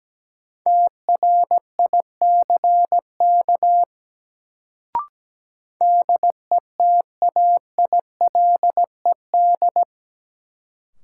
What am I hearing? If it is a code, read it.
Morse code, heard as a keyed tone, about 17 wpm, 700 Hz. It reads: TRICK DETAILED
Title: Sentences from Top 1500 Words - New Word Drill Alt - 17wpm